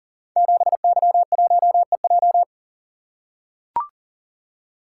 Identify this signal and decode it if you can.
Morse code, heard as a keyed tone, about 39 wpm, 690 Hz. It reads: 7Y1EJ
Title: Code Groups - 5 Letter:Numbers - 40wpm